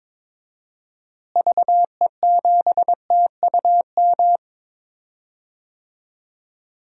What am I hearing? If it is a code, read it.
Morse code, heard as a keyed tone, about 22 wpm, 700 Hz. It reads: VE7TUM
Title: DXCC-100 Call Signs - 6 Characters - Repeated 2x - 22wpm